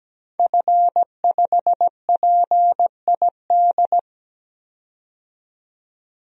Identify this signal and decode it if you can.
Morse code, heard as a keyed tone, about 17 wpm, 700 Hz. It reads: F5PID
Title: DXCC-100 Call Signs - 5 Characters - 17wpm